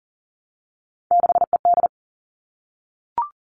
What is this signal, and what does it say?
Morse code, heard as a keyed tone, about 40 wpm, 700 Hz. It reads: 6ED
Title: All Cut Numbers - 3-Digit 40wpm